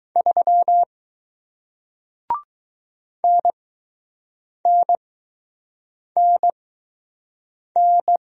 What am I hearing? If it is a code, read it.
Morse code, heard as a keyed tone, about 20 wpm, spaced as for 6 wpm, 700 Hz. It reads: 3 NNNN